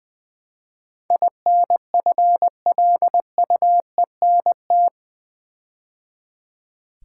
Morse code, heard as a keyed tone, about 20 wpm, 700 Hz. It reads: INFLUENT
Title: Build A Word - 4 Letters At A Time - 11-Letter Words - 20wpm